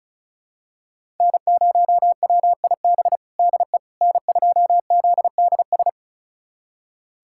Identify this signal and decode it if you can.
Morse code, heard as a keyed tone, about 34 wpm, 700 Hz. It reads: N0WIB DE N2ZDS
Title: USA Callsign DE Callsign - 5 Characters - 35wpm